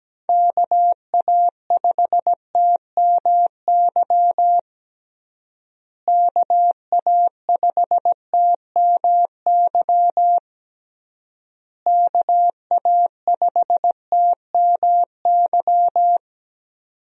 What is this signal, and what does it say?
Morse code, heard as a keyed tone, about 17 wpm, 700 Hz. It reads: KA5TMY KA5TMY KA5TMY
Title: USA Call Signs - 6 Characters - Repeated 3x - 17wpm